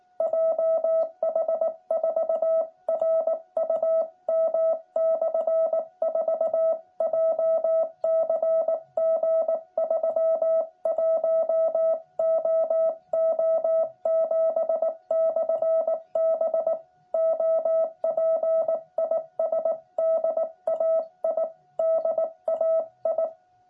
Morse code, heard as a keyed tone, about 19 wpm, 650 Hz. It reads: JH4RUM/4 JCG 31OO7/B OP IS DAI DAI
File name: jh8cbh / 2021cw